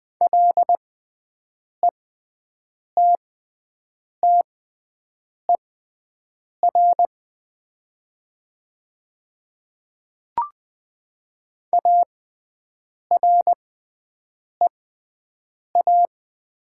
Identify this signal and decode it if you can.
Morse code, heard as a keyed tone, about 20 wpm, spaced as for 7 wpm, 700 Hz. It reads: LETTER AREA
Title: 6x Character Spacing - Top 500 Words - 20wpm